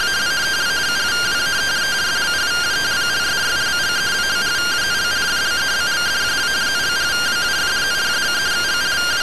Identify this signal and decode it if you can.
Two-tone teleprinter signal, about 170 Hz shift.